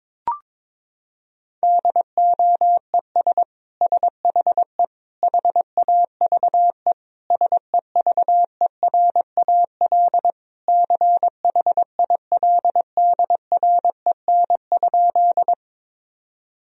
Morse code, heard as a keyed tone, about 22 wpm, 700 Hz. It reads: DOES SHE HAVE SEVERAL CHILDREN?
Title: Sentences from Top 1000 Words - Review Part 2 - 22wpm